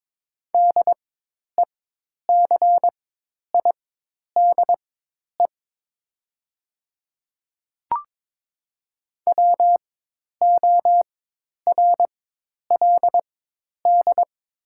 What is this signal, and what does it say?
Morse code, heard as a keyed tone, about 22 wpm, spaced as for 10 wpm, 700 Hz. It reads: DECIDE WORLD
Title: 4x Character Spacing - Top 500 Words - 22wpm